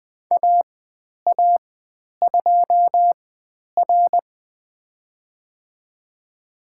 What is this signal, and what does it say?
Morse code, heard as a keyed tone, about 20 wpm, spaced as for 10 wpm, 700 Hz. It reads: AA2R
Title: 102 Lesson 17g - Letter u - Call Signs 20wpm